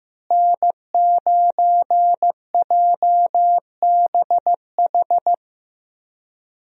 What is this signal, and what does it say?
Morse code, heard as a keyed tone, about 15 wpm, 700 Hz. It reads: N9JBH
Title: USA Call Signs - 5 Characters - Repeated 2x - 15wpm